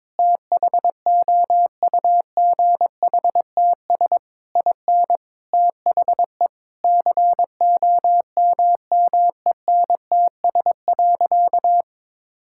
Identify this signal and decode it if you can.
Morse code, heard as a keyed tone, about 22 wpm, 700 Hz. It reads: THOUGHTS IN THE COMMENTS.
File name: practice / 22wpm / Sentences with 8-Word Phrases - 22wpm